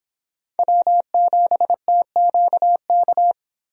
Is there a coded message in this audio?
Yes: Morse code from 0.6 to 3.3 s, heard as a keyed tone, about 24 wpm, 700 Hz. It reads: W7TQK